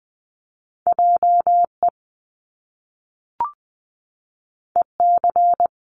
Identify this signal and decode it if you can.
Morse code, heard as a keyed tone, about 20 wpm, 700 Hz. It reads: JE EC
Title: Top Two Letter Combinations in Words 20wpm